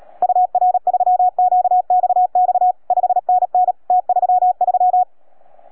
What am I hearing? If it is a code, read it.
Morse code, heard as a keyed tone, about 37 wpm, 730 Hz. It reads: UR3QXX 5NN T33